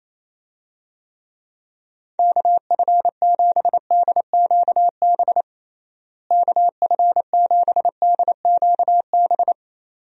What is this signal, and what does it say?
Morse code, heard as a keyed tone, about 28 wpm, 700 Hz. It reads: KF7DQB KF7DQB
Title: USA Call Signs - 6 Characters - Repeated 2x - 28wpm